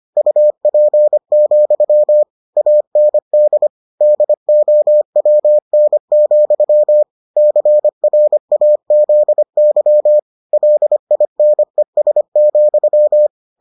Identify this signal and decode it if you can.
Morse code, heard as a keyed tone, about 25 wpm, 600 Hz. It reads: UP, AND DOWN, CRAZY LINES,